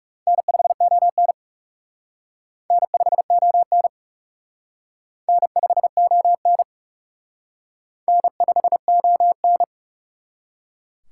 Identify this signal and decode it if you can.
Morse code, heard as a keyed tone, about 37 wpm, 700 Hz. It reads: N5ON N5ON N5ON N5ON
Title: Speed Racing - USA Call Signs 30wpm